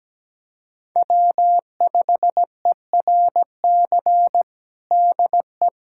Morse code, heard as a keyed tone, about 17 wpm, 700 Hz. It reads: W5ERC DE
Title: USA Callsign DE Callsign 17wpm